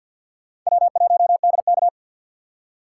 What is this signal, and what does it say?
Morse code, heard as a keyed tone, about 51 wpm, 700 Hz. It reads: W1DK
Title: USA Call Signs - 4 Characters - 3x Character Spacing - 50wpm